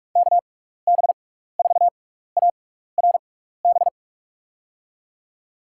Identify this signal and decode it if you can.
Morse code, heard as a keyed tone, about 44 wpm, spaced as for 15 wpm, 700 Hz. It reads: KB4ARB
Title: USA Call Signs - 6x Character Spacing - 45wpm